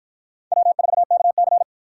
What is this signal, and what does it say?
Morse code, heard as a keyed tone, about 53 wpm, 700 Hz. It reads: W4KC